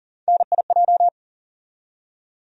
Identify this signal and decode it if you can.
Morse code, heard as a keyed tone, about 40 wpm, 700 Hz. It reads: NIJ
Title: USA Call Sign Postfixes - 3 Letters 40wpm